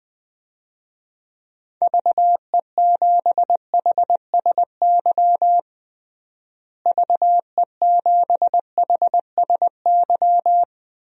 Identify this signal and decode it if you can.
Morse code, heard as a keyed tone, about 20 wpm, 700 Hz. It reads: VE7HSY VE7HSY
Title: DXCC-100 Call Signs - 6 Characters - Repeated 2x - 20wpm